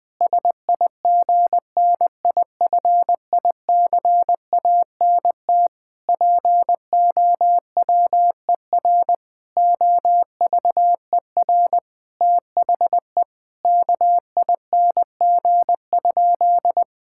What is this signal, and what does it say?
Morse code, heard as a keyed tone, about 20 wpm, 700 Hz. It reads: SIGNIFICANT POWER OVER THE KING?
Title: Sentences from Top 1000 Words - Review Part 6 - 20wpm